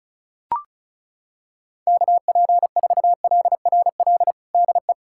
Morse code, heard as a keyed tone, about 33 wpm, 690 Hz. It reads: KP4LRL DE
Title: USA Callsign DE Callsign 35wpm